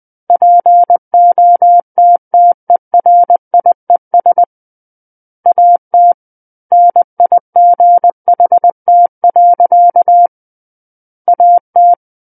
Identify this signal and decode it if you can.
Morse code, heard as a keyed tone, about 20 wpm, 700 Hz. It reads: POTTERIES AT NIGHT. AT
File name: 20 WPM